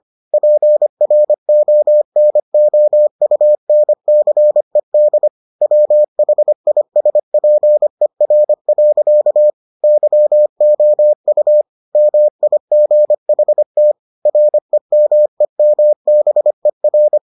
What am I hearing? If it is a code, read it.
Morse code, heard as a keyed tone, about 25 wpm, 600 Hz. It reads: PRONOUNCED WHISPER. YOU MIGHT REMEMBER